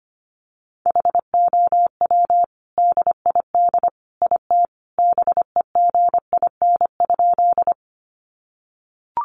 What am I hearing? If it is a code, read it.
Morse code, heard as a keyed tone, about 25 wpm, 700 Hz. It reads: HOW DID IT BEGIN?
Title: Sentences from Top 300 Words 25wpm